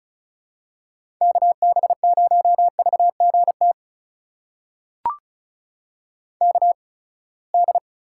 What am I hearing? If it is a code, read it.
Morse code, heard as a keyed tone, about 36 wpm, spaced as for 24 wpm, 700 Hz. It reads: KB0VGT KD
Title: USA Call Signs - 6 Characters - 8x Character Spacing - 35wpm